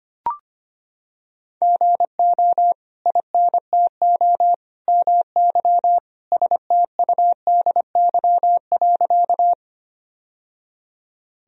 Morse code, heard as a keyed tone, about 25 wpm, 700 Hz. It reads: GO INTO MY STUDY.
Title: Sentences from Top 1000 Words - Review Part 1 - 25wpm